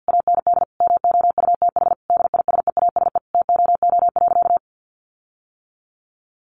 Morse code, heard as a keyed tone, about 50 wpm, 700 Hz. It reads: AND MOUTH DISEASE TOO.
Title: Sentences from Top 1500 Words 50wpm